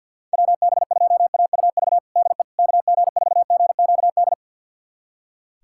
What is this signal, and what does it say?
Morse code, heard as a keyed tone, about 46 wpm, 700 Hz. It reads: WB2AUV DE KG4GQD
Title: USA Callsign DE Callsign - 6 Characters - 50wpm